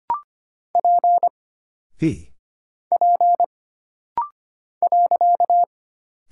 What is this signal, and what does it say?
Morse code, heard as a keyed tone, about 25 wpm, 700 Hz. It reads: P P .
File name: Single Character (Letter:Number:Punctuation:Prosign) - Rapid-Fire - 25wpm